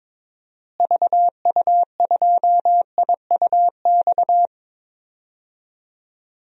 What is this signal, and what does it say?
Morse code, heard as a keyed tone, about 21 wpm, 700 Hz. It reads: VU2IUX
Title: DXCC-100 Call Signs - 6 Characters - Repeated 2x - 22wpm